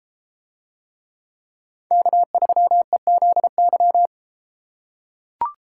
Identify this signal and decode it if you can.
Morse code, heard as a keyed tone, about 32 wpm, 700 Hz. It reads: K3EZY